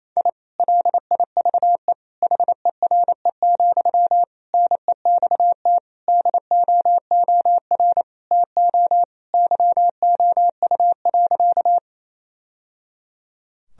Morse code, heard as a keyed tone, about 28 wpm, 700 Hz. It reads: I LIVE HERE, NEXT DOOR TO YOU.
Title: Sentences from Top 1000 Words - Review Part 1 - 28wpm